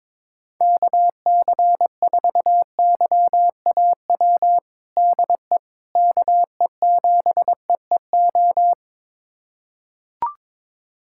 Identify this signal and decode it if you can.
Morse code, heard as a keyed tone, about 22 wpm, 700 Hz. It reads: KC4YAW DE KE7EEO
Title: USA Callsign DE Callsign 22wpm